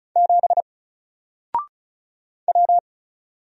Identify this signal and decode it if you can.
Morse code, heard as a keyed tone, about 36 wpm, 700 Hz. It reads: 7 W
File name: Single-Letter-Number-35wpm